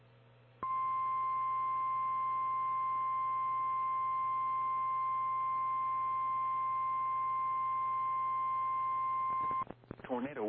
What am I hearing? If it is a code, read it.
Emergency Alert System (SAME) tones (the 1050 Hz attention tone), with speech. It is heard from 0.6 to 9.7 s.
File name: eas / KKPRKXPN / KKPRKXPN